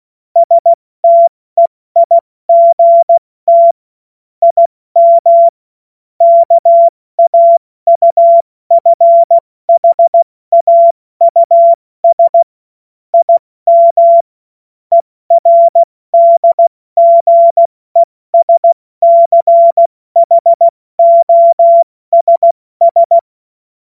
Morse code, heard as a keyed tone, about 16 wpm, spaced as for 14 wpm, 680 Hz. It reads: STEIGT IM KAUFHAUS IM ERDGESCHOSS
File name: Morsekurs / Witze / Witz 31. August 2021, 70 F80